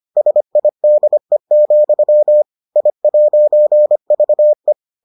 Morse code, heard as a keyed tone, about 25 wpm, 600 Hz. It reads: SIDE, I'VE